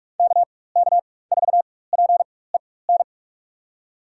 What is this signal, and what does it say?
Morse code, heard as a keyed tone, about 46 wpm, spaced as for 21 wpm, 700 Hz. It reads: KK4PEN
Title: USA Call Signs - 6 Characters - 4x Character Spacing - 45wpm